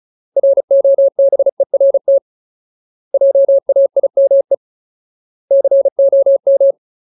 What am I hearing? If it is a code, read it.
Morse code, heard as a keyed tone, about 34 wpm, 550 Hz. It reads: ROBERT JAIME COM
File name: Ultimate Word List / 35WPM Ultimate